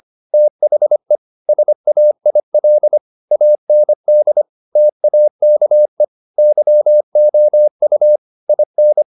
Morse code, heard as a keyed tone, about 25 wpm, 600 Hz. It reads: THE SAIL AND TAKE YOU IN
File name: the old man and the sea part 1 25wpm